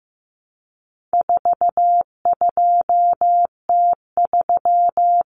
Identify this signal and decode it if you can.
Morse code, heard as a keyed tone, about 15 wpm, 700 Hz. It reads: 42T3
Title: Common Cut Numbers - 4-Digit 15wpm